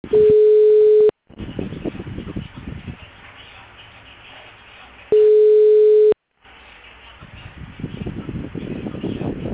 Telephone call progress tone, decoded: ringback tone

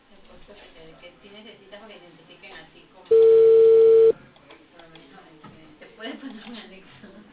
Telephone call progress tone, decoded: ringback tone

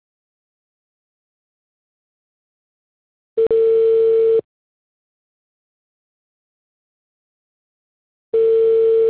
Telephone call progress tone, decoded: ringback tone

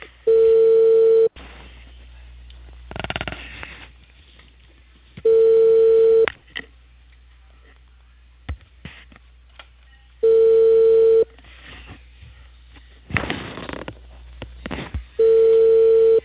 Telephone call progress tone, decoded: ringback tone

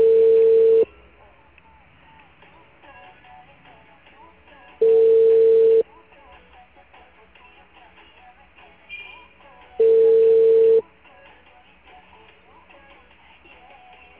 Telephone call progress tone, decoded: ringback tone

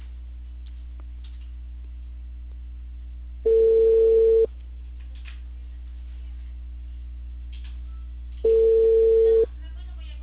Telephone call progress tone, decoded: ringback tone

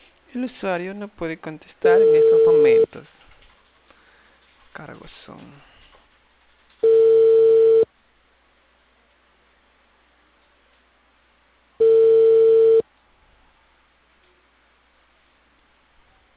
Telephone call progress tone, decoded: ringback tone

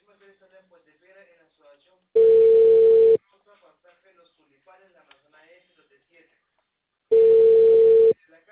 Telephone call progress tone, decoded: ringback tone

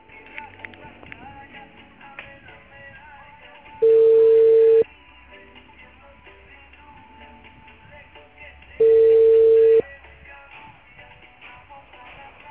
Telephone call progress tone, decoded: ringback tone

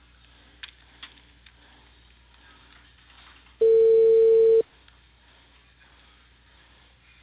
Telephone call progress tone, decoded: ringback tone

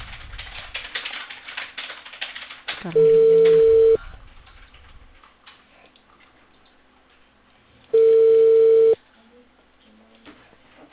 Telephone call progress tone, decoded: ringback tone

